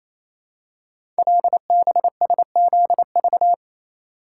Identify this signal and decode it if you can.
Morse code, heard as a keyed tone, about 27 wpm, 700 Hz. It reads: LBSZV
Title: Code Groups - 5 Letters - 28wpm